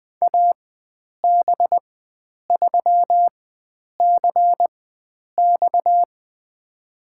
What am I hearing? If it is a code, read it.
Morse code, heard as a keyed tone, about 20 wpm, spaced as for 9 wpm, 700 Hz. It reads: AB3CX
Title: USA Call Signs - 5 Characters - 4x Character Spacing - 20wpm